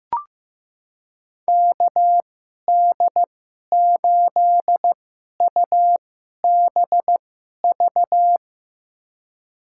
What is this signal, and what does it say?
Morse code, heard as a keyed tone, about 15 wpm, spaced as for 11 wpm, 700 Hz. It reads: KD8UBV
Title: USA Call Signs - 2x Character Spacing - 15wpm